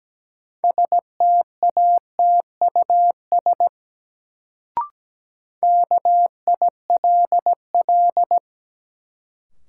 Morse code, heard as a keyed tone, about 17 wpm, 700 Hz. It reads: STATUS KILL
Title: Sentences from Top 1400 Words - New Word Drill Alt - 17wpm